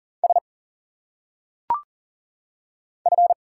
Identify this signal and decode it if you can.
Morse code, heard as a keyed tone, about 40 wpm, 700 Hz. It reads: S F
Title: Single Letters - Phonetic Pronunciation 40wpm